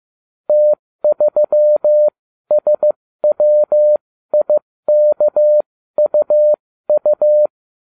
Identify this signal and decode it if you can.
Morse code, heard as a keyed tone, about 15 wpm, spaced as for 13 wpm, 600 Hz. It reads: T3SWIKUU